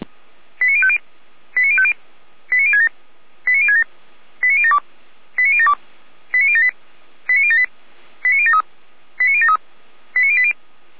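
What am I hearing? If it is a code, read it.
Five-tone selective call (ZVEI), heard as quick runs of five tones, tones of about 70 ms. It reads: 89050 89050 89067 89067 89072 89072 89078 89078 89073 89073 89080
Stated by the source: ZVEI five-tone call received by SDR.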